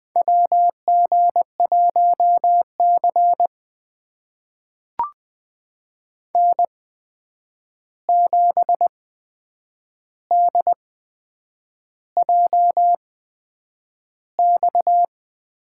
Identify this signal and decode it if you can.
Morse code, heard as a keyed tone, about 20 wpm, spaced as for 8 wpm, 700 Hz. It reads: WG1C N7DJX
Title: USA Call Signs - 8x Character Spacing - 20wpm